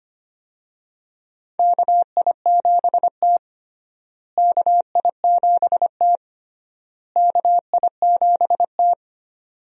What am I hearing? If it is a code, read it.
Morse code, heard as a keyed tone, about 25 wpm, 700 Hz. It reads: KI7T KI7T KI7T
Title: USA Call Signs - 4 Characters - Repeated 3x - 25wpm